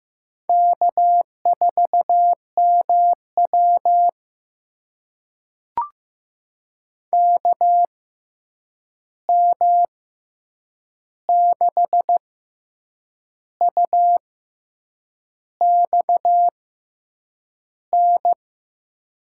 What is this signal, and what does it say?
Morse code, heard as a keyed tone, about 15 wpm, spaced as for 7 wpm, 700 Hz. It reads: K4MW KM6UXN
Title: USA Call Signs - 6x Character Spacing - 15wpm